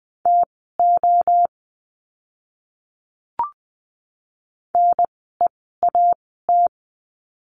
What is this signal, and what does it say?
Morse code, heard as a keyed tone, about 20 wpm, spaced as for 15 wpm, 700 Hz. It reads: TO NEAT
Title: 036 Lesson 7f - Letter s - Words 20wpm